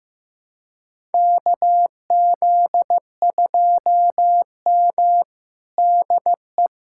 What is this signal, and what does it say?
Morse code, heard as a keyed tone, about 15 wpm, 700 Hz. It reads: KZ2M DE
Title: USA Callsign DE Callsign - 4 Characters - 15wpm